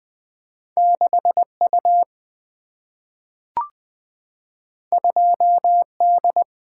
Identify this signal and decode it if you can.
Morse code, heard as a keyed tone, about 20 wpm, 700 Hz. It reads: 6U 2D
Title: Single-Letter Single-Number 20wpm